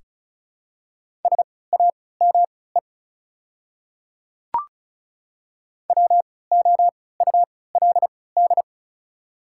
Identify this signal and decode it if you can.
Morse code, heard as a keyed tone, about 36 wpm, spaced as for 20 wpm, 700 Hz. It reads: SAME WOULD
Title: 3x Character Spacing - Top 200 Words - 35wpm